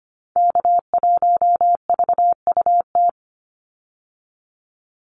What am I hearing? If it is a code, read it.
Morse code, heard as a keyed tone, about 25 wpm, 700 Hz. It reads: K1VUT
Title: Call Signs - 25wpm